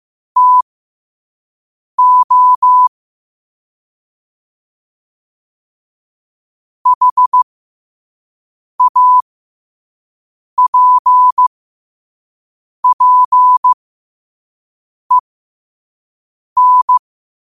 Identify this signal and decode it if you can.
Morse code, heard as a keyed tone, about 15 wpm, spaced as for 5 wpm, 1000 Hz. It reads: TO HAPPEN
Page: New quotes every day in morse code at 5 Words per minute.